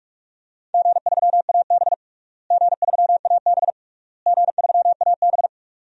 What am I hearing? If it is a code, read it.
Morse code, heard as a keyed tone, about 46 wpm, 700 Hz. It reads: G3AB G3AB G3AB
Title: DXCC-100 Call Signs - 4 Characters - Repeated 3x - 45wpm